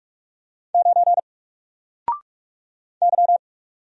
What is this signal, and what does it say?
Morse code, heard as a keyed tone, about 44 wpm, 700 Hz. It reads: 9 Y
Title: Single Letter-Number 45wpm